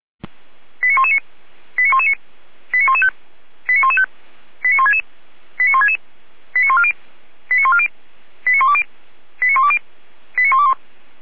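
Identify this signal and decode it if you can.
Five-tone selective call (ZVEI), heard as quick runs of five tones, tones of about 70 ms. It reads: 89119 89119 89115 89115 89166 89166 89140 89140 89120 89120 89121
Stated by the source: ZVEI five-tone selective call received on SDR, 455 channel path.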